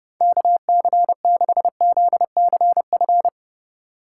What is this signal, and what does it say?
Morse code, heard as a keyed tone, about 38 wpm, 700 Hz. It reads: KC6ZCF